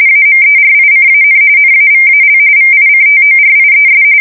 RTTY, about 45.45 baud. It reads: AMERICAN NAVAL VICTORIES
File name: fsk170 / usnavy2